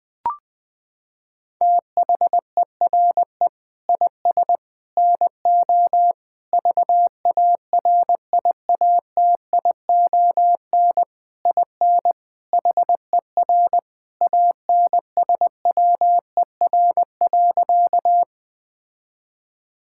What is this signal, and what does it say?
Morse code, heard as a keyed tone, about 20 wpm, 700 Hz. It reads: THERE IS NO VARIATION IN HER ANSWER.
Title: Sentences from Top 1600 Words 20wpm